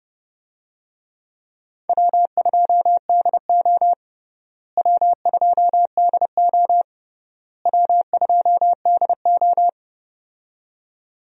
Morse code, heard as a keyed tone, about 30 wpm, 700 Hz. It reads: W2DO W2DO W2DO
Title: USA Call Signs - 4 Characters - Repeated 3x - 30wpm